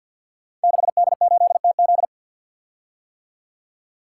Morse code, heard as a keyed tone, about 46 wpm, 700 Hz. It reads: 6D8TZ
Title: Code Groups - 5 Letter:Numbers - 50wpm